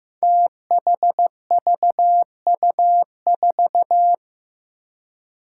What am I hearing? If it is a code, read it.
Morse code, heard as a keyed tone, about 15 wpm, 700 Hz. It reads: THVU4
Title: Five Challenging Letter-Number Combinations 15wpm